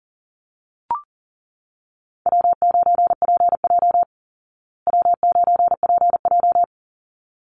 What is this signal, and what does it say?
Morse code, heard as a keyed tone, about 40 wpm, 700 Hz. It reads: W9PJ W9PJ
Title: Easy Call Signs Repeated 2x - 40wpm